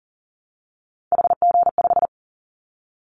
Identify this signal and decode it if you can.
Morse code, heard as a keyed tone, about 40 wpm, 700 Hz. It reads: HG5
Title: Callsign Prefixes for Top 25 DXCC - 40wpm